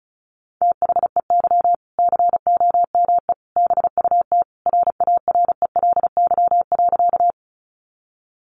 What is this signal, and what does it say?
Morse code, heard as a keyed tone, about 34 wpm, 700 Hz. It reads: THEY COME BUT RARELY.
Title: Sentences from Top 1400 Words 35wpm